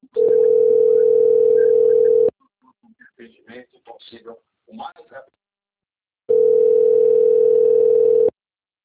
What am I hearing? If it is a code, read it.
Ringback tone (North America), 440 + 480 Hz, with speech.
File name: rg-1600-unknown- - - .45505